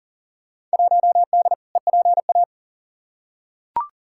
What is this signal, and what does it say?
Morse code, heard as a keyed tone, about 40 wpm, 700 Hz. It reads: 1D EPA
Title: Field Day Exchanges - 40wpm